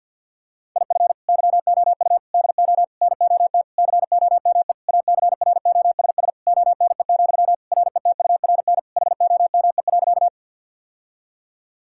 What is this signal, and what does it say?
Morse code, heard as a keyed tone, about 52 wpm, 700 Hz. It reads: IF YOU DO NOT COME ACROSS ONE, RETURN HOME.
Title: Sentences from Top 1000 Words - Review Part 5 - 50wpm